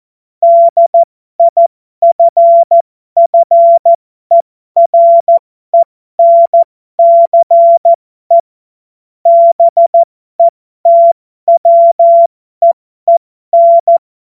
Morse code, heard as a keyed tone, about 14 wpm, spaced as for 12 wpm, 680 Hz. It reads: DIFFERENCE BETWEEN
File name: Morsekurs / Witze